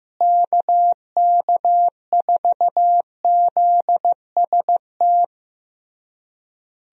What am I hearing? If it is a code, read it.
Morse code, heard as a keyed tone, about 15 wpm, 700 Hz. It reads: KK4ZST
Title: USA Call Signs - 6 Characters - Repeated 3x - 15wpm